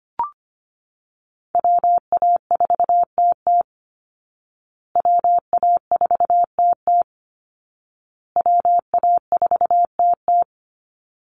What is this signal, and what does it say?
Morse code, heard as a keyed tone, about 25 wpm, 700 Hz. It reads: WA4TT WA4TT WA4TT
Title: Call Signs Repeated 2x - 25wpm